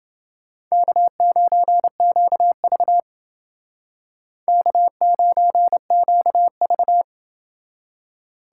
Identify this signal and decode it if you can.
Morse code, heard as a keyed tone, about 28 wpm, 700 Hz. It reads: K9QV K9QV